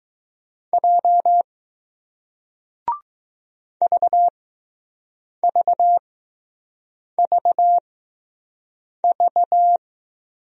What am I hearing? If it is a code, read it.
Morse code, heard as a keyed tone, about 20 wpm, spaced as for 6 wpm, 700 Hz. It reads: J VVVV